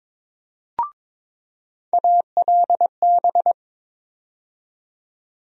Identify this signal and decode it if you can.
Morse code, heard as a keyed tone, about 22 wpm, 700 Hz. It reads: ALB
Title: USA Call Sign Postfixes - 3 Letters 22wpm